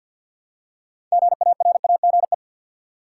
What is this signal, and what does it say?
Morse code, heard as a keyed tone, about 53 wpm, 700 Hz. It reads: GARAGE
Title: Six Letter Words - 50wpm